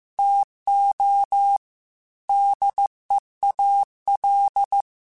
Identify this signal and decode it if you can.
Morse code, heard as a keyed tone, about 15 wpm, 800 Hz. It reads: TO DEAL
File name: Technical Subjects / Technical Subjects-mp3s / AND Gate @13